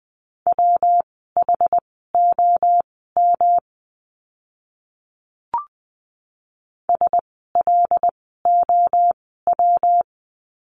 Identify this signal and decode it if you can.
Morse code, heard as a keyed tone, about 20 wpm, spaced as for 14 wpm, 700 Hz. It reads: WHOM SLOW
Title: 115 Lesson 19f - Letter w - Words 20wpm